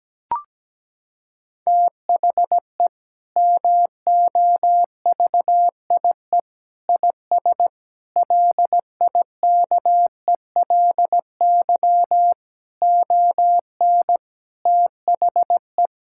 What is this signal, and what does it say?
Morse code, heard as a keyed tone, about 17 wpm, 700 Hz. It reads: THE MOVIE IS LIKELY ON THE
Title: Sentences from Top 1000 Words - Review Part 5 - 17wpm